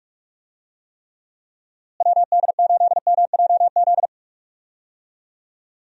Morse code, heard as a keyed tone, about 45 wpm, 700 Hz. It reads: WD8MJZ
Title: USA Call Signs - 6 Characters - Repeated 3x - 45wpm